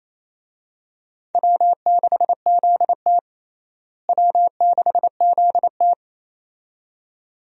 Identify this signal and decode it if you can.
Morse code, heard as a keyed tone, about 28 wpm, 700 Hz. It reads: W6ZT W6ZT
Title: USA Call Signs - 4 Characters - Repeated 2x - 28wpm